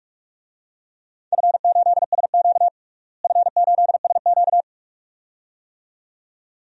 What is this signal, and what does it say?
Morse code, heard as a keyed tone, about 46 wpm, 700 Hz. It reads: F8SQ F8SQ
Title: DXCC-100 Call Signs - 4 Characters - Repeated 2x - 45wpm